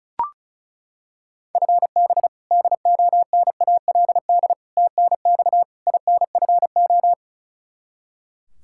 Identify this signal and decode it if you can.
Morse code, heard as a keyed tone, about 36 wpm, 700 Hz. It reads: FB DONALD TNX INFO
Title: Practice QSO - Exchange 3 - More Info - 35wpm